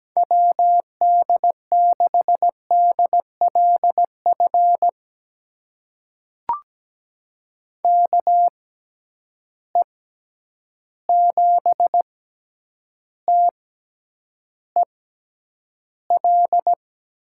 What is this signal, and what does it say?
Morse code, heard as a keyed tone, about 17 wpm, spaced as for 9 wpm, 700 Hz. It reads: WD6DLF KE7TEL
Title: USA Call Signs - 6x Character Spacing - 17wpm